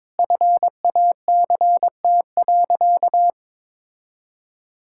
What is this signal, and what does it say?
Morse code, heard as a keyed tone, about 22 wpm, 700 Hz. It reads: FACT.
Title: Sentences from Top 1000 Words - Review Part 6 - 22wpm